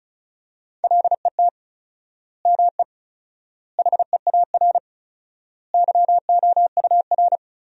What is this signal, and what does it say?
Morse code, heard as a keyed tone, about 36 wpm, 700 Hz. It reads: LET ME HEAR YOUR
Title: 4x Word-Spacing - Sentences from Top 300 Words - 35wpm